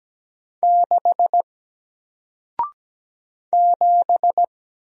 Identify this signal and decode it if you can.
Morse code, heard as a keyed tone, about 17 wpm, 700 Hz. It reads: 6 7
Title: Single Letter-Number Alt 17wpm